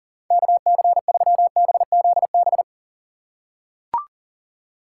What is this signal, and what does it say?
Morse code, heard as a keyed tone, about 37 wpm, 700 Hz. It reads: KC3BZB
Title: USA Call Signs - 6 Characters - 4x Character Spacing - 40wpm